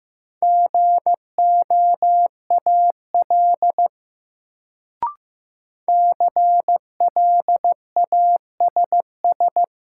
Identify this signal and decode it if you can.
Morse code, heard as a keyed tone, about 15 wpm, 700 Hz. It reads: GOAL CLASS
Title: Sentences from Top 1400 Words - New Word Drill Alt - 15wpm